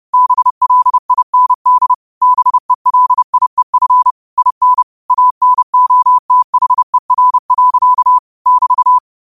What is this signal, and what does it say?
Morse code, heard as a keyed tone, about 30 wpm, 1000 Hz. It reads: BLIND BELIEF IN ANOTHER. <BT>
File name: qotdfiles / 30wpm